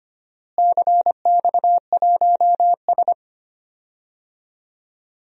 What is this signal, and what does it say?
Morse code, heard as a keyed tone, about 25 wpm, 700 Hz. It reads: CX1S
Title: DXCC-100 Call Signs - 4 Characters - Repeated 3x - 25wpm